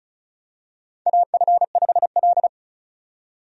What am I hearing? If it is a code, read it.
Morse code, heard as a keyed tone, about 36 wpm, 700 Hz. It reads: AF5L
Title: USA Call Signs - 4 Characters - 35wpm